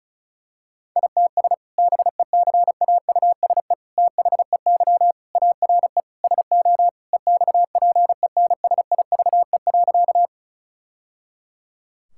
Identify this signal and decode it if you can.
Morse code, heard as a keyed tone, about 36 wpm, 700 Hz. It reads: ITS BECAUSE THEY ARE SO EXPENSIVE.
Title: Sentences from Top 1000 Words - Review Part 6 - 35wpm